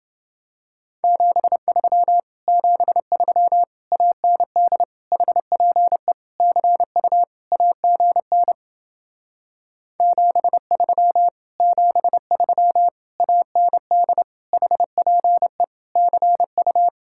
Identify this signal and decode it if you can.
Morse code, heard as a keyed tone, about 29 wpm, 700 Hz. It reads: 73 73 AND HPE CU AGN 73 73 AND HPE CU